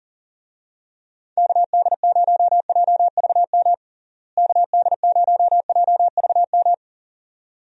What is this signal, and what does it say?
Morse code, heard as a keyed tone, about 40 wpm, 700 Hz. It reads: KD0JVM KD0JVM
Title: USA Call Signs - 6 Characters - Repeated 2x - 40wpm